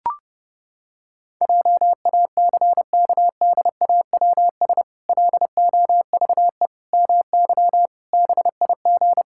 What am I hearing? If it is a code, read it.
Morse code, heard as a keyed tone, about 32 wpm, 700 Hz. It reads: JACKDAWS LOVE MY BIG
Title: Pangram Challenge v2 - 30wpm